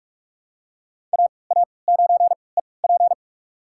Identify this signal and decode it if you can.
Morse code, heard as a keyed tone, about 44 wpm, spaced as for 25 wpm, 700 Hz. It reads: AA9EP
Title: USA Call Signs - 5 Characters - 3x Character Spacing - 45wpm